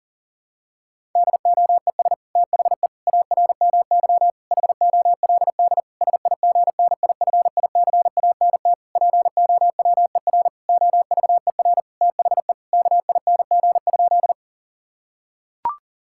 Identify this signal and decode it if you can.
Morse code, heard as a keyed tone, about 40 wpm, 690 Hz. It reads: DOES THE ARMY HOLD SIGNIFICANT POWER OVER THE KING?
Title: Sentences from Top 1000 Words - Review Part 6 - 40wpm